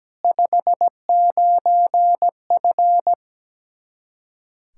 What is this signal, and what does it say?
Morse code, heard as a keyed tone, about 17 wpm, 700 Hz. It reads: 59F
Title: Practice QSO - Exchange 2 - Info - 17wpm